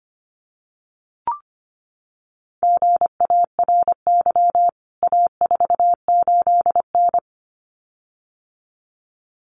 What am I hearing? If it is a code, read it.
Morse code, heard as a keyed tone, about 25 wpm, 700 Hz. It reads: GARY A48N
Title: CWT Exchange 25wpm